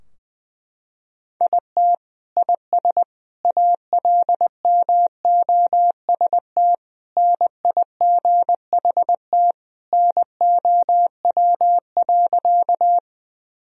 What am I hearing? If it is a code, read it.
Morse code, heard as a keyed tone, about 21 wpm, 700 Hz. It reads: IT IS ALMOST NIGHT NOW.
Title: Sentences from Top 1000 Words - Review Part 4 - 20wpm